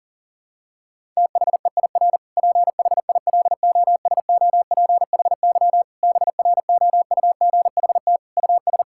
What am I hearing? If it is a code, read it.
Morse code, heard as a keyed tone, about 40 wpm, 700 Hz. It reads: THEIR PHILOSOPHY BROUGHT US
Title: Sentences from Top 1000 Words - Review Part 6 - 40wpm